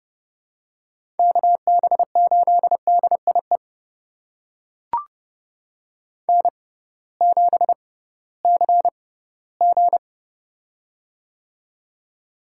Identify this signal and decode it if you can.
Morse code, heard as a keyed tone, about 30 wpm, spaced as for 17 wpm, 700 Hz. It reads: KB8DIE N7CG
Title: USA Call Signs - 6x Character Spacing - 30wpm